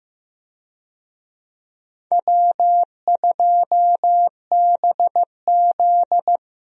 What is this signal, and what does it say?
Morse code, heard as a keyed tone, about 15 wpm, 700 Hz. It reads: W2BZ
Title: USA Call Signs - 4 Characters - 15wpm